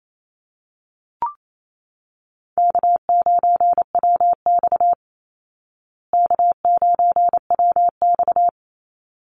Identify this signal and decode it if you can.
Morse code, heard as a keyed tone, about 28 wpm, 700 Hz. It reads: K9WX K9WX
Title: Call Signs Repeated 3x - 28wpm